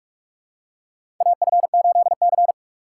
Morse code, heard as a keyed tone, about 46 wpm, 700 Hz. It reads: AF8C
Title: USA Call Signs - 4 Characters - 3x Character Spacing - 45wpm